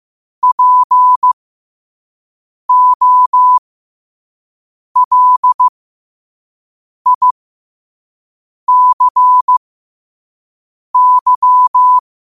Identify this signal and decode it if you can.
Morse code, heard as a keyed tone, about 15 wpm, spaced as for 5 wpm, 1000 Hz. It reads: POLICY
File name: qotdfiles / 5wpm